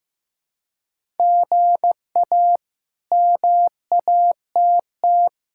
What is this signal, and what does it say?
Morse code, heard as a keyed tone, about 15 wpm, 700 Hz. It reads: GA MATT
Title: K1USN SST QSO Practice - 15wpm